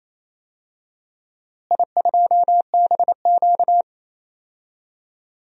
Morse code, heard as a keyed tone, about 28 wpm, 700 Hz. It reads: I2BQ
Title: DXCC-100 Call Signs - 4 Characters - Repeated 3x - 28wpm